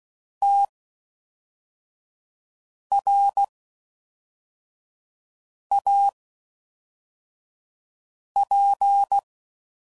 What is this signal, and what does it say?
Morse code, heard as a keyed tone, about 16 wpm, spaced as for 4 wpm, 800 Hz. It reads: TRAP